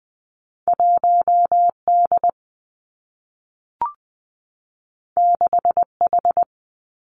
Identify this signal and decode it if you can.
Morse code, heard as a keyed tone, about 19 wpm, 700 Hz. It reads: 1 D 6 H
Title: Two Challenging Letter-Number Combinations 20wpm